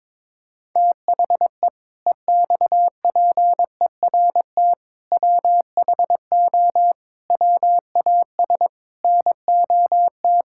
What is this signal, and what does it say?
Morse code, heard as a keyed tone, about 22 wpm, 700 Hz. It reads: THE EXPERT WHO WAS NOT
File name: Sentences with 7-Word Phrases - 22wpm